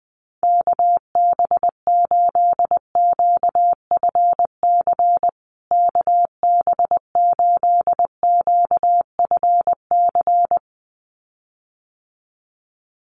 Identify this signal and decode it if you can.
Morse code, heard as a keyed tone, about 20 wpm, 700 Hz. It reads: KB8QFC KB8QFC
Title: USA Call Signs Repeated 2x 20wpm